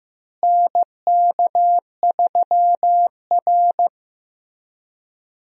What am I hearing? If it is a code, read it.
Morse code, heard as a keyed tone, about 15 wpm, 700 Hz. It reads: NK3R
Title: USA Call Signs - 4 Characters - Repeated 3x - 15wpm